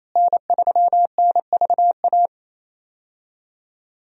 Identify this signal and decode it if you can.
Morse code, heard as a keyed tone, about 28 wpm, 700 Hz. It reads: N3NVA
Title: USA Call Signs - 5 Characters - Repeated 3x - 28wpm